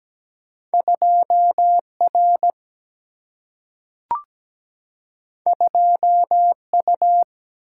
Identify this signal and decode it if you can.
Morse code, heard as a keyed tone, about 17 wpm, 700 Hz. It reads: 2R 2U
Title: Single-Letter Single-Number 17wpm